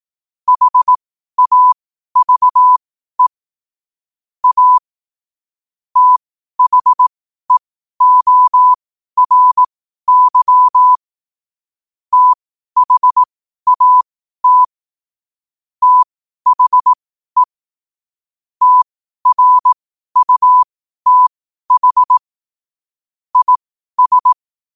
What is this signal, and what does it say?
Morse code, heard as a keyed tone, about 18 wpm, spaced as for 12 wpm, 1000 Hz. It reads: HAVE A THEORY THAT THE TRUTH IS
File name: 12wpm